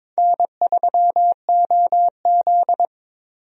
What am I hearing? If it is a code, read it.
Morse code, heard as a keyed tone, about 22 wpm, 700 Hz. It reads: N3OZ
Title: USA Call Signs - 4 Characters - Repeated 2x - 22wpm